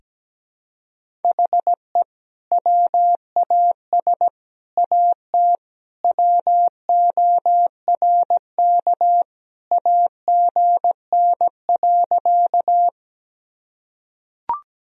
Morse code, heard as a keyed tone, about 17 wpm, 700 Hz. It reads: HE WAS AT WORK AGN.
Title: Sentences from Top 200 Words and Abbreviations 17wpm